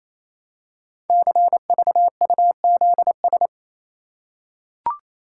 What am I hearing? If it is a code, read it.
Morse code, heard as a keyed tone, about 27 wpm, 700 Hz. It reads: CVUZS
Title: Code Groups - 5 Letters - 28wpm